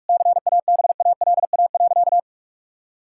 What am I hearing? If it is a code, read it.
Morse code, heard as a keyed tone, about 46 wpm, 700 Hz. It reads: CABALA.
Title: Pangram Challenge v2 - 45wpm